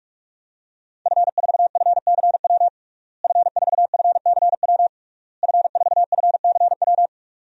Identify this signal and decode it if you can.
Morse code, heard as a keyed tone, about 41 wpm, 700 Hz. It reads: F4FCW F4FCW F4FCW
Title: DXCC-100 Call Signs - 5 Characters - Repeated 3x - 45wpm